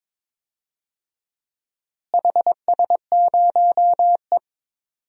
Morse code, heard as a keyed tone, about 21 wpm, 700 Hz. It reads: HS0E
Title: DXCC-100 Call Signs - 4 Characters - Repeated 3x - 22wpm